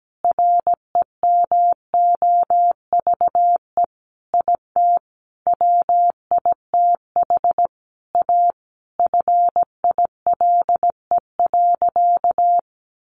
Morse code, heard as a keyed tone, about 17 wpm, 700 Hz. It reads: REMOVE IT WITH A FILE.
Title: Sentences from Top 1200 Words 17wpm